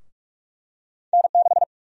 Morse code, heard as a keyed tone, about 43 wpm, 700 Hz. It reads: N6
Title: USA Call Sign Prefixes - Repeated 2x - 45wpm